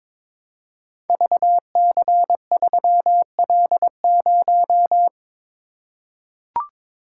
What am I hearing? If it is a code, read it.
Morse code, heard as a keyed tone, about 22 wpm, 700 Hz. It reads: VC3L0
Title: Code Groups - 5 Letter:Numbers - 22wpm